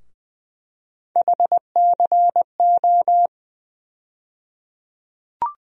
Morse code, heard as a keyed tone, about 20 wpm, 700 Hz. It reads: HCO
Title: USA Call Sign Postfixes - 3 Letters 20wpm